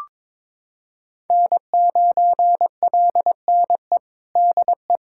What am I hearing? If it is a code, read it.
Morse code, heard as a keyed tone, about 22 wpm, 700 Hz. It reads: N9LNE DE
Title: USA Callsign DE Callsign - 5 Characters - 22wpm